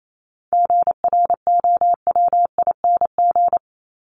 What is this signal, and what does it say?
Morse code, heard as a keyed tone, about 28 wpm, 700 Hz. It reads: GROWING
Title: ING Suffix - Top 100 Words with Base Words 28wpm